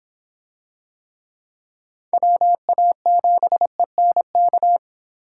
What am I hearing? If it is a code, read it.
Morse code, heard as a keyed tone, about 24 wpm, 700 Hz. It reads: WA7ENK